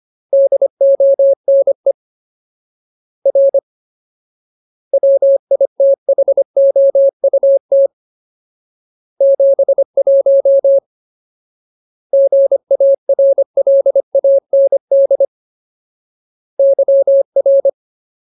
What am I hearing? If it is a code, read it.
Morse code, heard as a keyed tone, about 25 wpm, 550 Hz. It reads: DONE R WITHOUT 71 GARLAND YR
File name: Ultimate Word List / 25WPM Ultimate